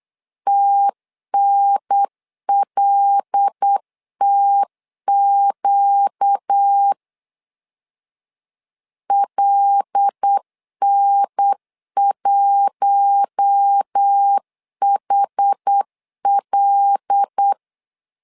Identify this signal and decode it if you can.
Morse code, heard as a keyed tone, about 8 wpm, 800 Hz. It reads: TNLTQ LN1HL
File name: LEZIONI / 5 LBVC / 5 L 8wpm 2831